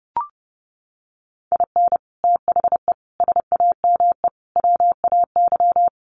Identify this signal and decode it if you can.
Morse code, heard as a keyed tone, about 30 wpm, 700 Hz. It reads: IN THE SAME WAY
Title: Sets of 5 Words 30wpm